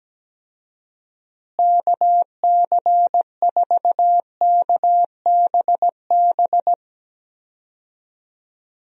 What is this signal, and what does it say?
Morse code, heard as a keyed tone, about 17 wpm, 700 Hz. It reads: KC4KBB
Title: USA Call Signs - 6 Characters - Repeated 3x - 17wpm